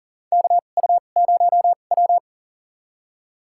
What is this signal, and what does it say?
Morse code, heard as a keyed tone, about 40 wpm, spaced as for 29 wpm, 700 Hz. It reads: KU0W
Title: USA Call Signs - 4 Characters - 2x Character Spacing - 40wpm